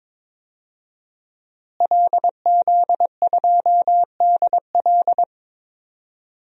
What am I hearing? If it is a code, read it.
Morse code, heard as a keyed tone, about 22 wpm, 700 Hz. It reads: LZ2DL
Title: DXCC-100 Call Signs - 5 Characters - Repeated 3x - 22wpm